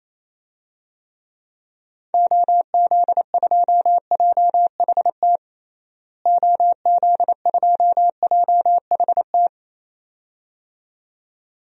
Morse code, heard as a keyed tone, about 28 wpm, 700 Hz. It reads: OZ2JHT OZ2JHT
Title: DXCC-100 Call Signs - 6 Characters - Repeated 2x - 28wpm